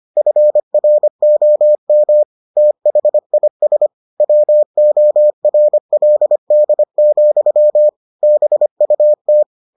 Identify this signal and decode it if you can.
Morse code, heard as a keyed tone, about 25 wpm, 600 Hz. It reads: FROM THIS WORLD, BUT